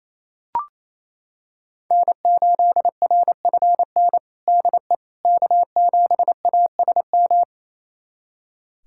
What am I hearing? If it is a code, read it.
Morse code, heard as a keyed tone, about 28 wpm, 700 Hz. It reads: N8RFN DE K7ASM
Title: USA Callsign DE Callsign - 5 Characters - 28wpm